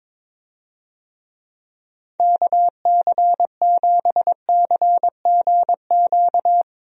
Morse code, heard as a keyed tone, about 21 wpm, 700 Hz. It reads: KC7CGQ
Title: USA Call Signs - 6 Characters - Repeated 2x - 22wpm